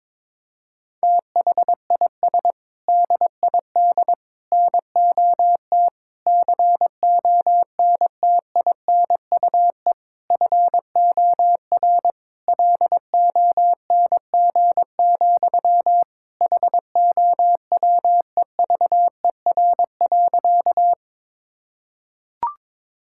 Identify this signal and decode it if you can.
Morse code, heard as a keyed tone, about 22 wpm, 700 Hz. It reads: THIS DID NOT CONTINUE FOR LONG, HOWEVER.
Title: Sentences from Top 1000 Words - Review Part 6 - 22wpm